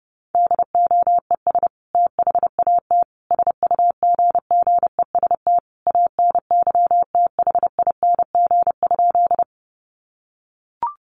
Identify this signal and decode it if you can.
Morse code, heard as a keyed tone, about 30 wpm, 700 Hz. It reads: DOES THAT SUGGEST ANYTHING?
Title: Sentences from Top 1300 Words 30wpm